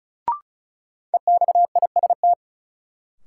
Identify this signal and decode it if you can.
Morse code, heard as a keyed tone, about 34 wpm, 700 Hz. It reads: EXIST
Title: Sentences from Top 1000 Words - Review Part 5 - New Word Drill - 35wpm